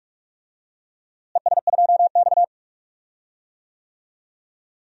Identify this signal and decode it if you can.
Morse code, heard as a keyed tone, about 47 wpm, 700 Hz. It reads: ES2X
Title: DXCC-100 Call Signs - 4 Characters - Repeated 2x - 45wpm